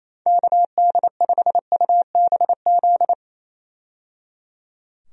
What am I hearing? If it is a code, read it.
Morse code, heard as a keyed tone, about 27 wpm, 700 Hz. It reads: KD5UBZ
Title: USA Call Signs - 6 Characters - 28wpm